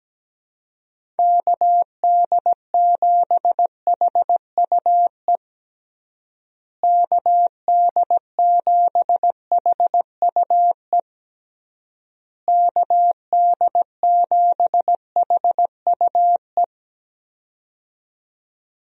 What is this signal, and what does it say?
Morse code, heard as a keyed tone, about 17 wpm, 700 Hz. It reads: KD7HUE KD7HUE KD7HUE
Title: USA Call Signs - 6 Characters - Repeated 3x - 17wpm